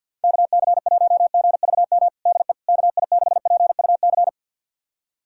Morse code, heard as a keyed tone, about 50 wpm, 700 Hz. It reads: KC1GVM DE KI6PUC
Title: USA Callsign DE Callsign 50wpm